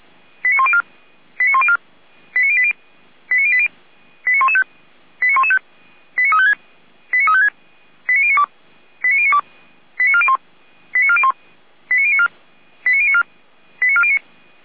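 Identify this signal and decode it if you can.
Five-tone selective call (ZVEI), heard as quick runs of five tones, tones of about 70 ms. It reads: 89104 89104 89080 89080 89115 89115 89356 89356 89092 89092 89401 89401 89094 89094 89409
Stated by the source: ZVEI five-tone selective call 89025, received on SDR.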